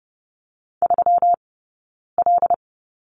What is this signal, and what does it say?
Morse code, heard as a keyed tone, about 30 wpm, 700 Hz. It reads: 3 L
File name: Lesson 20d - Number 3 - Two Characters 30wpm